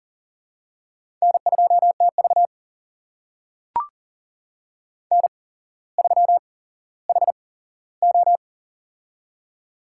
Morse code, heard as a keyed tone, about 40 wpm, spaced as for 17 wpm, 700 Hz. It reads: N2TV N3HO
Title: USA Call Signs - 4 Characters - 8x Character Spacing - 40wpm